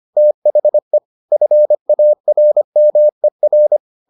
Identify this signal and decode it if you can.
Morse code, heard as a keyed tone, about 25 wpm, 600 Hz. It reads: THE FARMER